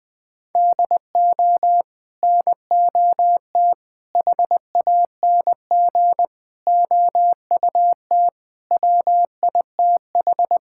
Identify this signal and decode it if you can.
Morse code, heard as a keyed tone, about 21 wpm, 700 Hz. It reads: DO NOT HANG OUT WITH
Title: Sentences from Top 1600 Words 20wpm